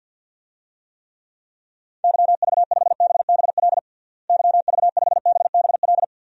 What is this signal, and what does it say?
Morse code, heard as a keyed tone, about 52 wpm, 700 Hz. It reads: YV5BBL YV5BBL
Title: DXCC-100 Call Signs - 6 Characters - Repeated 3x - 50wpm